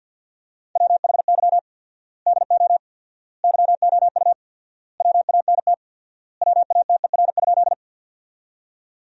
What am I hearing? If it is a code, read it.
Morse code, heard as a keyed tone, about 50 wpm, 700 Hz. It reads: WHY DO YOU WANT WATER?
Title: 4x Word-Spacing - Sentences from Top 200 Words - 50wpm